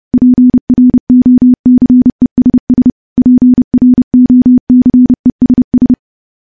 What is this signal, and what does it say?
Morse code, heard as a keyed tone, about 30 wpm, 250 Hz. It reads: PROCESS PROCESS
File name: morse / morse-training-30WPM